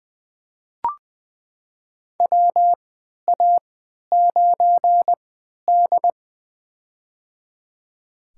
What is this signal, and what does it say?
Morse code, heard as a keyed tone, about 20 wpm, spaced as for 11 wpm, 700 Hz. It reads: WA9D
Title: USA Call Signs - 3x Character Spacing - 20wpm